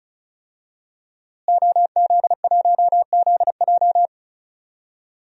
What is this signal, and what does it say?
Morse code, heard as a keyed tone, about 34 wpm, 700 Hz. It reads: OZ1ZJ
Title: DXCC-100 Call Signs - 5 Characters - Repeated 3x - 35wpm